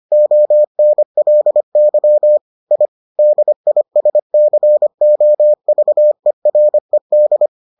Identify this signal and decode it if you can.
Morse code, heard as a keyed tone, about 25 wpm, 600 Hz. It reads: ONLY I DISCOVERED